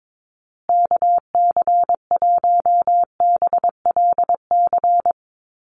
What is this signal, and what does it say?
Morse code, heard as a keyed tone, about 22 wpm, 700 Hz. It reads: KC1BLC
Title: USA Call Signs Repeated 3x 22wpm